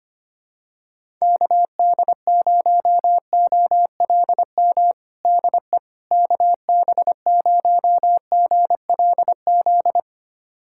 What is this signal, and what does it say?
Morse code, heard as a keyed tone, about 25 wpm, 700 Hz. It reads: KD0OLM DE KB0GLZ
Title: USA Callsign DE Callsign - 6 Characters - 25wpm